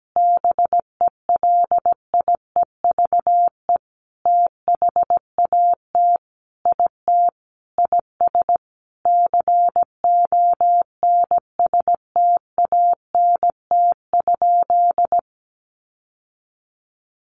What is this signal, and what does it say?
Morse code, heard as a keyed tone, about 17 wpm, 700 Hz. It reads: BELIEVE THAT IT IS CONSTANT?
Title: Sentences from Top 1500 Words 17wpm